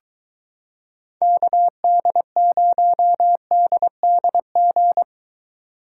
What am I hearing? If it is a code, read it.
Morse code, heard as a keyed tone, about 22 wpm, 700 Hz. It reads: KD0DDG